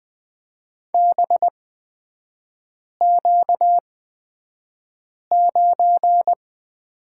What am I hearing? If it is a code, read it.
Morse code, heard as a keyed tone, about 20 wpm, spaced as for 5 wpm, 700 Hz. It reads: BQ9
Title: 236 Lesson 38e - Letter q - Three Characters 20wpm